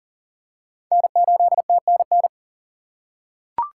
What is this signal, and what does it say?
Morse code, heard as a keyed tone, about 40 wpm, 700 Hz. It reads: N8TNN
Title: USA Call Signs - 2x Character Spacing - 40wpm